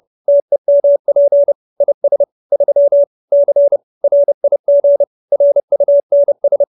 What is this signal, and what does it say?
Morse code, heard as a keyed tone, about 30 wpm, 570 Hz. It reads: TEMP IS 3 C RIG RUNS
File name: QSO310 30